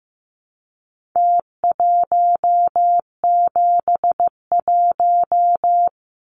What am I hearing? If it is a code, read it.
Morse code, heard as a keyed tone, about 15 wpm, 700 Hz. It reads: T171
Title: Numbers - Contest Style - 4-Digit 15wpm